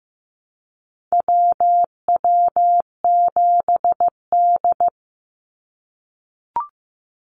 Morse code, heard as a keyed tone, about 15 wpm, 700 Hz. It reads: WW7D
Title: NA SOTA Activator Callsigns 15wpm